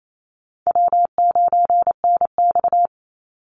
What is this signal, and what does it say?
Morse code, heard as a keyed tone, about 28 wpm, 700 Hz. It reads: W9NX
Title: Easy Call Signs - 28wpm